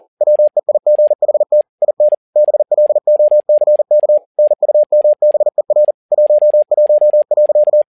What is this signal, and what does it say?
Morse code, heard as a keyed tone, about 40 wpm, 600 Hz. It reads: WEIGHT IN BLOCK NUMBER 11.
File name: learning center / tower trouble / tower touble 40wpm